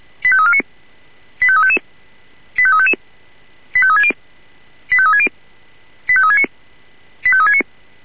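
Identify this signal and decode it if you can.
Five-tone selective call (ZVEI), heard as quick runs of five tones, tones of about 70 ms. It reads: 85349 85355 85355 85360 85360 85369 85369
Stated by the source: ZVEI five-tone call received by SDR.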